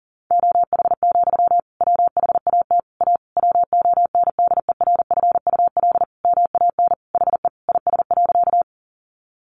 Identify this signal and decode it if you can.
Morse code, heard as a keyed tone, about 40 wpm, 700 Hz. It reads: OH, WHAT A WONDERFUL MAN HE IS.
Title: Sentences from Top 1100 Words 40wpm